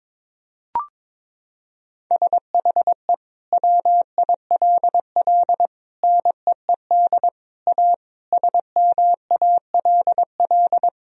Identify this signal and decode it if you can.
Morse code, heard as a keyed tone, about 22 wpm, 700 Hz. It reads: SHE WILL NEED A SMALL
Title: Sentences from Top 1000 Words - Review Part 2 - 22wpm